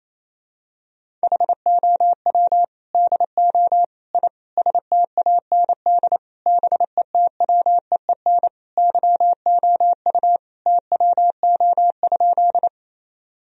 Morse code, heard as a keyed tone, about 28 wpm, 700 Hz. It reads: HOW DO I STAND BETWEEN YOU TWO?
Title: Sentences from Top 1000 Words - Review Part 1 - 28wpm